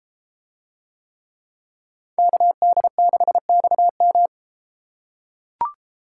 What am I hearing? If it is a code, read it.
Morse code, heard as a keyed tone, about 31 wpm, 700 Hz. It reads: KD6XM